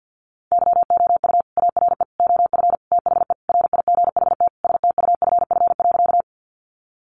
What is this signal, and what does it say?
Morse code, heard as a keyed tone, about 50 wpm, 700 Hz. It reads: YOU ARE OF THE RIGHT STUFF.
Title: Sentences from Top 1400 Words 50wpm